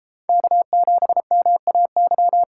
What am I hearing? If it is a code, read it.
Morse code, heard as a keyed tone, about 37 wpm, 700 Hz. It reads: K7MAY